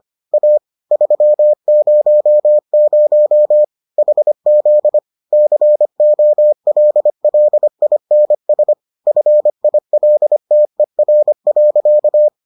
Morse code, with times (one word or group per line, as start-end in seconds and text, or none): 0.3-0.6 A
0.9-3.6 300
4.0-5.0 HZ
5.3-8.7 COLLINS
9.1-12.3 FILTER.